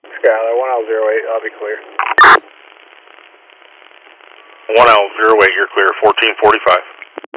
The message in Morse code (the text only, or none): none